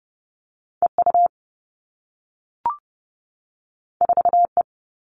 Morse code, heard as a keyed tone, about 30 wpm, 700 Hz. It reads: EU 4E
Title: All Cut Numbers - 2-Digit 30wpm